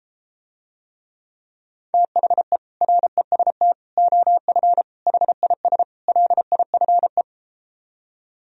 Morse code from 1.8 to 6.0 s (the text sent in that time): THE REST OF HIS